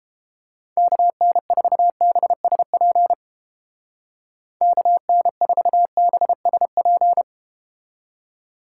Morse code, heard as a keyed tone, about 31 wpm, 700 Hz. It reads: KN4BSP KN4BSP